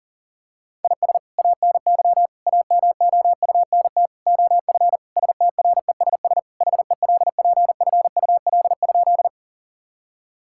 Morse code, heard as a keyed tone, about 40 wpm, 700 Hz. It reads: IS ANY AMOUNT OF STRESS HELPFUL?
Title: Sentences from Top 1000 Words - Review Part 6 - 40wpm